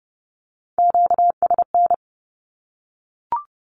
Morse code, heard as a keyed tone, about 30 wpm, 700 Hz. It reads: QSN
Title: Q-Codes 30wpm